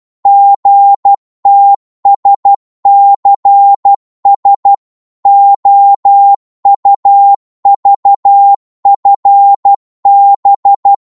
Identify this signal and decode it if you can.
Morse code, heard as a keyed tone, about 12 wpm, 800 Hz. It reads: GTSCS OUVFB
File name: Morse / ltr-12wpm